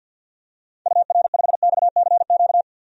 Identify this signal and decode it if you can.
Morse code, heard as a keyed tone, about 52 wpm, 700 Hz. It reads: UR5XCQ
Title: DXCC-100 Call Signs - 6 Characters - 50wpm